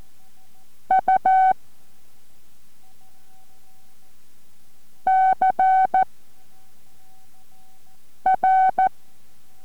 Morse code, heard as a keyed tone, about 14 wpm, 760 Hz. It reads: U C R